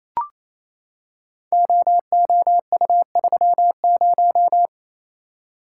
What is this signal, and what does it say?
Morse code, heard as a keyed tone, about 29 wpm, 700 Hz. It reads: OOU30
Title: Code Groups - 5 Letter:Numbers - 28wpm